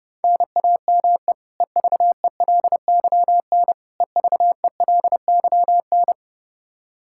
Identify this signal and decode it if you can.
Morse code, heard as a keyed tone, about 30 wpm, 700 Hz. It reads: NAME EVELYN EVELYN
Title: Practice QSO - Exchange 1 - Beginning - 30wpm